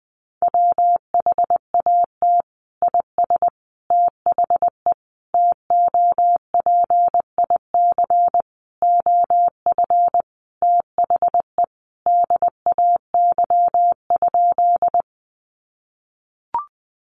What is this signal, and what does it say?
Morse code, heard as a keyed tone, about 20 wpm, 700 Hz. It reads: WHAT IS THE TOPIC OF THE DAY?
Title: Sentences from Top 1200 Words 20wpm